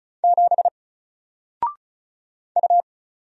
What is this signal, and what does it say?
Morse code, heard as a keyed tone, about 36 wpm, 700 Hz. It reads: 7 U
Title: Single Letter-Number Alt 35wpm